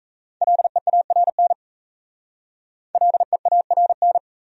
Morse code, heard as a keyed tone, about 34 wpm, 700 Hz. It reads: LEARN LEARN
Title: Speed-Racing Top 500 Words 28wpm